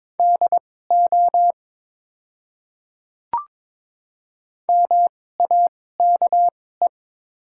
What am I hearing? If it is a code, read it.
Morse code, heard as a keyed tone, about 22 wpm, spaced as for 16 wpm, 690 Hz. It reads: DO MAKE
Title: 2x Character Spacing - Top 100 Words - 22wpm